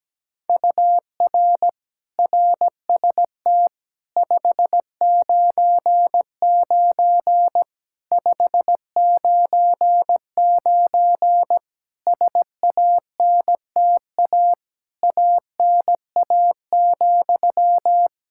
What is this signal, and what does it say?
Morse code, heard as a keyed tone, about 17 wpm, 700 Hz. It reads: UR RST 599 599 SANTA ANA,
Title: Practice QSO - Exchange 1 - Beginning - 17wpm